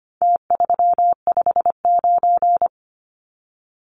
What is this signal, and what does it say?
Morse code, heard as a keyed tone, about 25 wpm, 700 Hz. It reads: T359
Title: Numbers - Contest Style - 4-Digit 25wpm